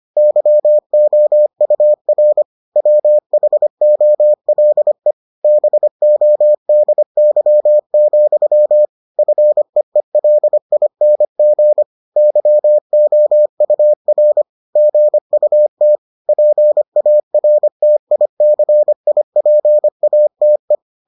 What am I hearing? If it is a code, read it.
Morse code, heard as a keyed tone, about 25 wpm, 600 Hz. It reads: YOUR WHOLE BODY, FEELING YOUR GUT PARTICIPATE